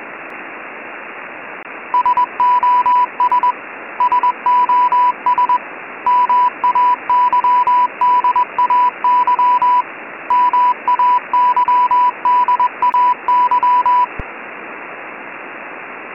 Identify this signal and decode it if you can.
Morse code, heard as a keyed tone, about 21 wpm, 990 Hz, with radio noise. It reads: SOS SOS MAYDAY MAYDAY
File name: Sos Mayday Morse Code 988 Hz 20wpm With Background Static